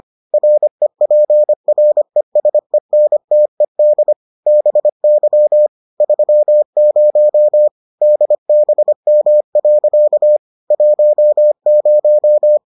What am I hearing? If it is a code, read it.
Morse code, heard as a keyed tone, about 25 wpm, 600 Hz. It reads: REPRESENTED BY 30 DBM. 10